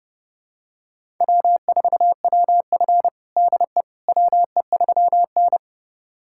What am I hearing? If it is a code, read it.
Morse code, heard as a keyed tone, about 32 wpm, 700 Hz. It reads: W4WF DE WE3N
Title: USA Callsign DE Callsign - 4 Characters - 30wpm